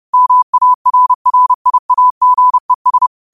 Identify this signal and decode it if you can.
Morse code, heard as a keyed tone, about 30 wpm, 1000 Hz. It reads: MARRIAGES